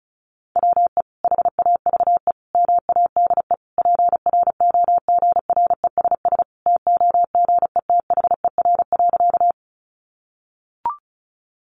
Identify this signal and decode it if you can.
Morse code, heard as a keyed tone, about 34 wpm, 700 Hz. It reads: WE HAVE MADE PROGRESS TOGETHER.
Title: Sentences from Top 1400 Words 35wpm